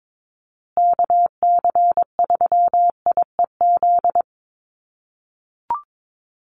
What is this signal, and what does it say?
Morse code, heard as a keyed tone, about 22 wpm, 700 Hz. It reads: KC3IEZ
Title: USA Call Signs Repeated 3x 22wpm